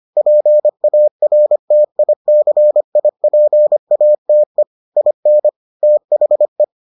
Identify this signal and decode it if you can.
Morse code, heard as a keyed tone, about 25 wpm, 600 Hz. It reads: PARTICIPATE IN THE